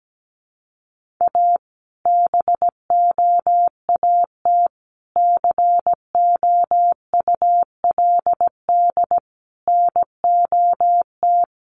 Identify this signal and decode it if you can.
Morse code, heard as a keyed tone, about 17 wpm, 700 Hz. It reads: A BOAT COULD NOT
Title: Sentences from Top 1400 Words 17wpm